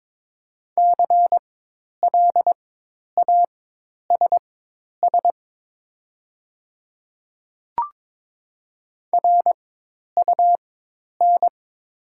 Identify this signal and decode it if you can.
Morse code, heard as a keyed tone, about 22 wpm, spaced as for 10 wpm, 700 Hz. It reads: CLASS RUN
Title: 4x Character Spacing - Top 500 Words - 22wpm